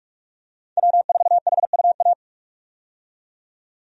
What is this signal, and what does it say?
Morse code, heard as a keyed tone, about 42 wpm, 700 Hz. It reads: W4HUA
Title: USA Call Signs - 5 Characters - 45wpm